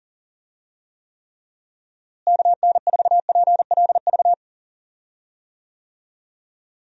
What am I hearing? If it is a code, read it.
Morse code, heard as a keyed tone, about 40 wpm, 700 Hz. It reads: KN4PLV
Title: USA Call Signs - 6 Characters - Repeated 3x - 40wpm